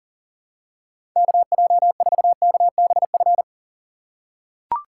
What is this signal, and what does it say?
Morse code, heard as a keyed tone, about 40 wpm, 700 Hz. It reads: KJ4KBF
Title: USA Call Signs - 3x Character Spacing - 40wpm